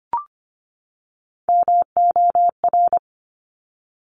Morse code, heard as a keyed tone, about 25 wpm, 700 Hz. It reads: MOR
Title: Top Three Letter Combinations in Words 25wpm